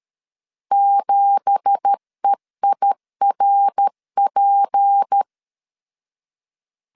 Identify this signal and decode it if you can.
Morse code, heard as a keyed tone, about 13 wpm, 800 Hz. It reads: 7EIRP
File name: CW / DET / 1 9 1500C S A 12wpm 3800